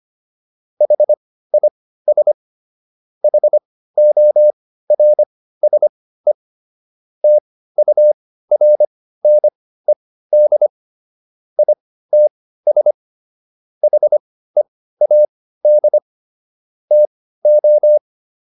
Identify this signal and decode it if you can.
Morse code, heard as a keyed tone, about 25 wpm, spaced as for 15 wpm, 600 Hz. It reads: HIS HORSE TURNED ITS HEAD TO